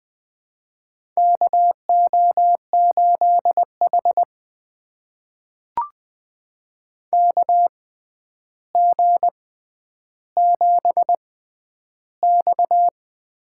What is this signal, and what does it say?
Morse code, heard as a keyed tone, about 20 wpm, spaced as for 10 wpm, 700 Hz. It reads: KO8H KG7X
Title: USA Call Signs - 4 Characters - 6x Character Spacing - 20wpm